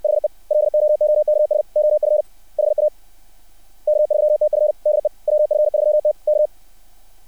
Morse code, heard as a keyed tone, about 60 wpm, 600 Hz. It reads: HE LOOKS OF HA K3TF DE WF3T K
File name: archived / CRSnet